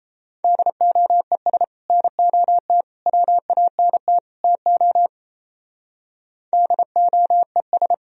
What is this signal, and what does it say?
Morse code, heard as a keyed tone, about 35 wpm, 700 Hz. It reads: DOES NOT WANT TO DOES
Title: Speed-Racing - Sets of 4 Words 28wpm